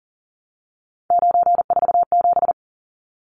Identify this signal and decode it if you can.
Morse code, heard as a keyed tone, about 40 wpm, 700 Hz. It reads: 947
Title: Numbers - 3-Digit 40wpm